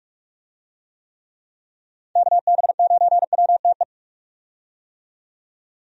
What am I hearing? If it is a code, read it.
Morse code, heard as a keyed tone, about 44 wpm, 700 Hz. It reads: KB9WTE
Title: USA Call Signs - 6 Characters - 45wpm